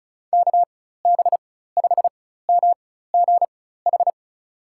Morse code, heard as a keyed tone, about 34 wpm, spaced as for 16 wpm, 700 Hz. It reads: KB5MGH
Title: USA Call Signs - 4x Character Spacing - 35wpm